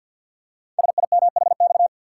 Morse code, heard as a keyed tone, about 46 wpm, 700 Hz. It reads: SIMHX
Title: Code Groups - 5 Letters - 50wpm